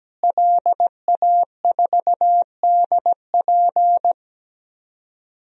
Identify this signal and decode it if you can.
Morse code, heard as a keyed tone, about 17 wpm, 700 Hz. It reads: LA4DP
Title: DXCC-100 Call Signs - 5 Characters - Repeated 2x - 17wpm